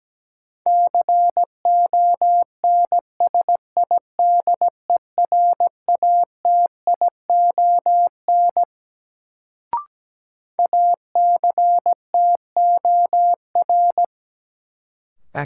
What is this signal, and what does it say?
Morse code, heard as a keyed tone, about 17 wpm, 700 Hz. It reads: CONSIDERATION ACTOR
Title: Sentences from Top 1500 Words - New Word Drill Alt - 17wpm